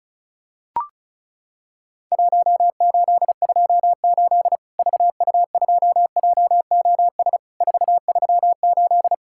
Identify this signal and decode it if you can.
Morse code, heard as a keyed tone, about 36 wpm, 700 Hz. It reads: 1828 VU2JOS 438
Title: Worked All Europe DX Contest - QTC Exchange - 35wpm